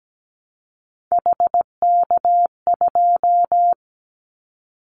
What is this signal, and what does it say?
Morse code, heard as a keyed tone, about 17 wpm, 700 Hz. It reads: HK2
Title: Callsign Prefixes for Top 100 DXCC - 17wpm